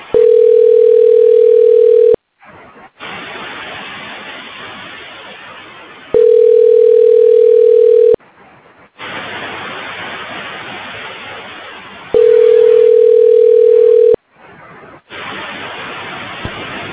Ringback tone (North America), 440 + 480 Hz.